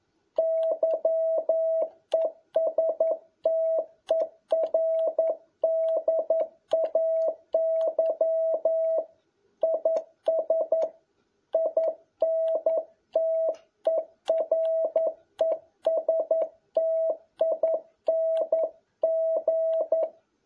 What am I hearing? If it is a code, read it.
Morse code, heard as a keyed tone, about 11 wpm, 650 Hz. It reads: YESTERDAY IS INTERESTING
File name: jh8cbh / 2021cw